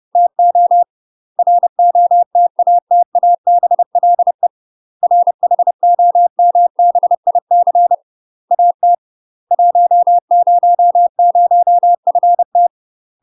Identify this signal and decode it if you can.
Morse code, heard as a keyed tone, about 30 wpm, 700 Hz. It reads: TO ROTATABLE RHOMBIC AT 100FT
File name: UK MASTER QSOs TRAINING / 30WPM / UK MASTER QSOs 0001